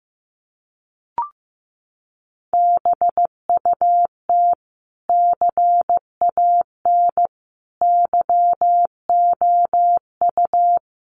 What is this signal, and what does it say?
Morse code, heard as a keyed tone, about 15 wpm, 700 Hz. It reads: BUT CAN YOU
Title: Sentences from Top 1400 Words 15wpm